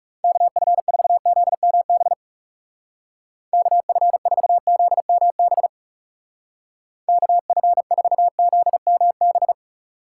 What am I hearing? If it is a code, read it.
Morse code, heard as a keyed tone, about 36 wpm, 700 Hz. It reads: KF4ZMB KF4ZMB KF4ZMB
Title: Speed Racing - USA 2x3 Call Signs 30wpm